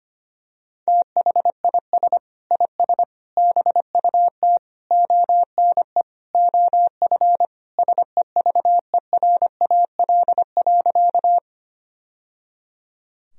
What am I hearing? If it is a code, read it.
Morse code, heard as a keyed tone, about 25 wpm, 700 Hz. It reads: THIS IS BUT ONE OF SEVERAL.
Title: Sentences from Top 1000 Words - Review Part 2 - 25wpm